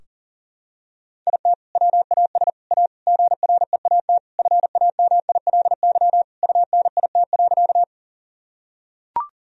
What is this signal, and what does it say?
Morse code, heard as a keyed tone, about 40 wpm, 700 Hz. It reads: IT WAS A GREAT FAMILY UNIT.
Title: Sentences from Top 1000 Words - Review Part 2 - 40wpm